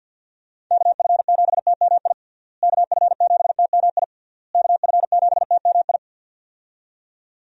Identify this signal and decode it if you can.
Morse code, heard as a keyed tone, about 47 wpm, 700 Hz. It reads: KF7TMI KF7TMI KF7TMI
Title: USA Call Signs - 6 Characters - Repeated 3x - 50wpm